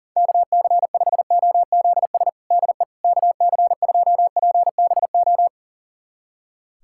Morse code, heard as a keyed tone, about 40 wpm, 700 Hz. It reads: KC5OZS DE KC2PBO
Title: USA Callsign DE Callsign 40wpm